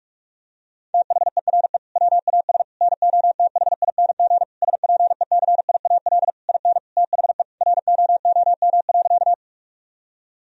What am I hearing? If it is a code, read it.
Morse code, heard as a keyed tone, about 46 wpm, 700 Hz. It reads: THERE WAS NOTHING SPECIAL IN THE ROOM.
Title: Sentences from Top 1000 Words - Review Part 3 - 45wpm